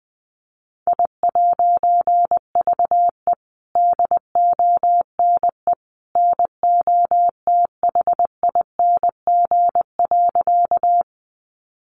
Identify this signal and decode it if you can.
Morse code, heard as a keyed tone, about 20 wpm, 700 Hz. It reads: I'VE DONE NOTHING.
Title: A Sound Of Thunder 20wpm